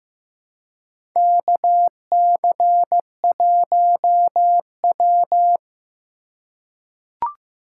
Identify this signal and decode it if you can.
Morse code, heard as a keyed tone, about 15 wpm, 700 Hz. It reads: KC1W
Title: USA Call Signs - 4 Characters - 3x Character Spacing - 15wpm